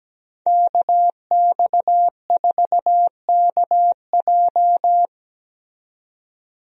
Morse code, heard as a keyed tone, about 17 wpm, 700 Hz. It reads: KX4KJ
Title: USA Call Signs - 5 Characters - 17wpm